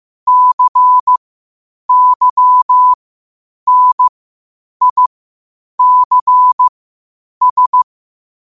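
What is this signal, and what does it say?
Morse code, heard as a keyed tone, about 15 wpm, spaced as for 8 wpm, 1000 Hz. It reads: CYNICS
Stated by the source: Morse code at 8 wpm.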